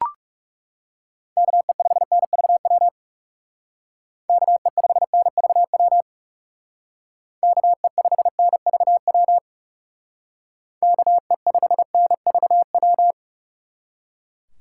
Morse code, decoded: KE5NVW KE5NVW KE5NVW KE5NVW

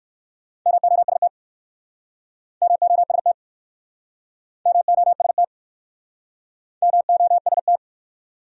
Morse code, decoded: MOST MOST MOST MOST

